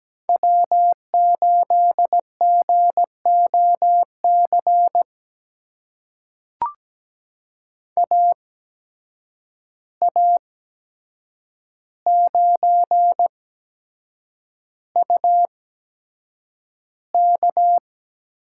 Morse code, heard as a keyed tone, about 17 wpm, spaced as for 7 wpm, 700 Hz. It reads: W8GOC AA9UK
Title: USA Call Signs - 5 Characters - 8x Character Spacing - 17wpm